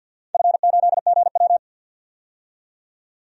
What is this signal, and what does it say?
Morse code, heard as a keyed tone, about 51 wpm, 700 Hz. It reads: F8GW
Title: DXCC-100 Call Signs - 4 Characters - 50wpm